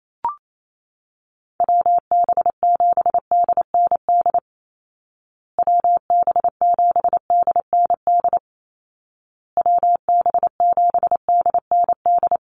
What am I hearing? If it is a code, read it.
Morse code, heard as a keyed tone, about 28 wpm, 700 Hz. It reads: WB7DND WB7DND WB7DND
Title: Call Signs Repeated 2x - 28wpm